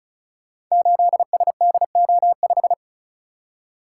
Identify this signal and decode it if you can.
Morse code, heard as a keyed tone, about 36 wpm, 700 Hz. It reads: 8SDO5
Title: Code Groups - 5 Letter:Numbers - 35wpm